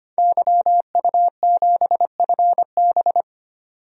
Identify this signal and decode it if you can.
Morse code, heard as a keyed tone, about 25 wpm, 700 Hz. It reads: YU7FB
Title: DXCC-100 Call Signs - 5 Characters - 25wpm